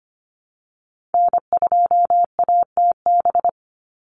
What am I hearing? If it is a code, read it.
Morse code, heard as a keyed tone, about 25 wpm, 700 Hz. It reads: N2ATB
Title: Call Signs Repeated 3x - 25wpm